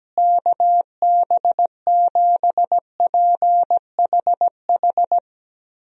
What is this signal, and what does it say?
Morse code, heard as a keyed tone, about 17 wpm, 700 Hz. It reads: KB7PHH
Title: USA Call Signs - 6 Characters - Repeated 3x - 17wpm